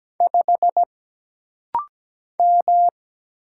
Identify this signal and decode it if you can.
Morse code, heard as a keyed tone, about 17 wpm, 700 Hz. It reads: 5 M
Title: Single Letter-Number Alt - Rapid-Fire 17wpm